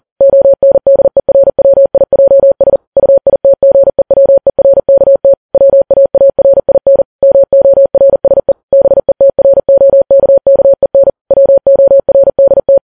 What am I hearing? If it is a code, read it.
Morse code, heard as a keyed tone, about 40 wpm, 550 Hz. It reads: ONDERWIJS UITGEWERKT WAARIN MORSE BETROKKEN WORDT
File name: Files Web tls / Morse on9fv / morse40kps